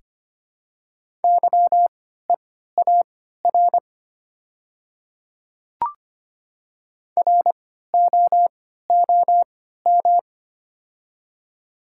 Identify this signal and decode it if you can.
Morse code, heard as a keyed tone, about 25 wpm, spaced as for 14 wpm, 700 Hz. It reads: YEAR ROOM
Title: 3x Character Spacing - Top 300 Words - 25wpm